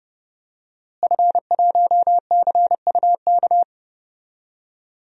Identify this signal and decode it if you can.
Morse code, heard as a keyed tone, about 32 wpm, 700 Hz. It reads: F1CUK
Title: DXCC-100 Call Signs - 5 Characters - Repeated 3x - 30wpm